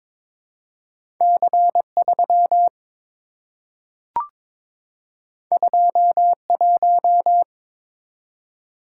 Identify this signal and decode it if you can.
Morse code, heard as a keyed tone, about 22 wpm, 700 Hz. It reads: C3 21
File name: practice / 22wpm / Two Letter-Number Combinations - 22wpm